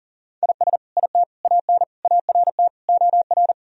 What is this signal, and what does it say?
Morse code, heard as a keyed tone, about 40 wpm, 700 Hz. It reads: IS IT AN ART OR
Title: Sentences from Top 1000 Words - Review Part 3 - 40wpm